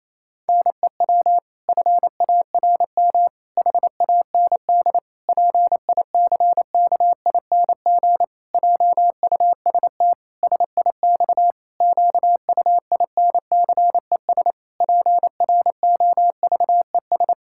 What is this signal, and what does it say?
Morse code, heard as a keyed tone, about 28 wpm, 700 Hz. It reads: NEW FARM HAND PICKING JUST SIX QUINCES PROVES
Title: Pangram Challenge v2 - 28wpm